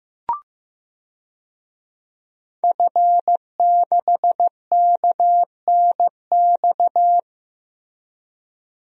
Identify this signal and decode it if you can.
Morse code, heard as a keyed tone, about 15 wpm, 700 Hz. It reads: F6KNX
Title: DXCC-100 Call Signs - 5 Characters - Repeated 3x - 15wpm